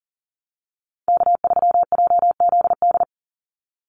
Morse code, heard as a keyed tone, about 38 wpm, 700 Hz. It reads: K3JZD
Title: NA SOTA Chaser Callsigns 40wpm